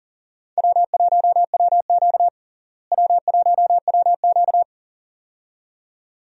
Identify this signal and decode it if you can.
Morse code, heard as a keyed tone, about 40 wpm, 700 Hz. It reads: W1WQ W1WQ
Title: USA Call Signs - 4 Characters - Repeated 2x - 40wpm